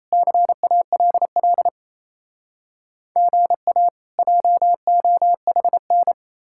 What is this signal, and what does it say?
Morse code, heard as a keyed tone, about 37 wpm, 700 Hz. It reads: CALL GA JOHN